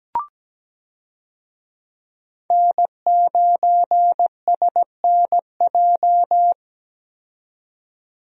Morse code, heard as a keyed tone, about 17 wpm, 700 Hz. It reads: N9SNJ
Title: USA Call Signs - 5 Characters - 17wpm